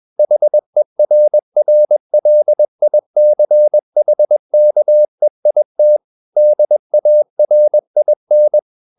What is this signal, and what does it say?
Morse code, heard as a keyed tone, about 21 wpm, 600 Hz. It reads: HERRLICHKEIT DARIN
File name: marienkind2 21wpm